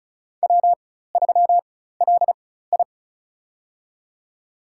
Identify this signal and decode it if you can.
Morse code, heard as a keyed tone, about 34 wpm, spaced as for 16 wpm, 700 Hz. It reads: W3LI
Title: USA Call Signs - 4 Characters - 4x Character Spacing - 35wpm